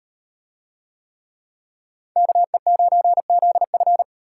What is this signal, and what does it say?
Morse code, heard as a keyed tone, about 38 wpm, 700 Hz. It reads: KE9ZF